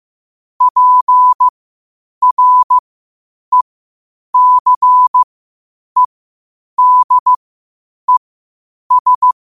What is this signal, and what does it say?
Morse code, heard as a keyed tone, about 15 wpm, spaced as for 8 wpm, 1000 Hz. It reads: PRECEDES
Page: qotd-at-8-wpm